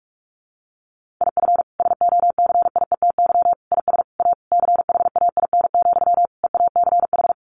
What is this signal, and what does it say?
Morse code, heard as a keyed tone, about 44 wpm, 700 Hz. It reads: IF SOCIETY IS A CHAIN, EACH
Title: Sentences from Top 1500 Words 45wpm